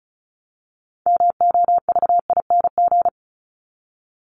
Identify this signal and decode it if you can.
Morse code, heard as a keyed tone, about 34 wpm, 700 Hz. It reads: MOVING
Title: ING Suffix - Top 100 Words with Base Words 35wpm